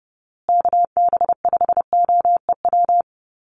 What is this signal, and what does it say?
Morse code, heard as a keyed tone, about 30 wpm, 700 Hz. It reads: KB5OEW
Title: USA Call Signs Repeated 3x 30wpm